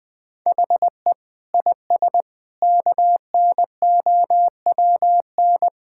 Morse code, heard as a keyed tone, about 20 wpm, 700 Hz. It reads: HE IS KNOWN
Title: Sentences with 5-Word Phrases - 20wpm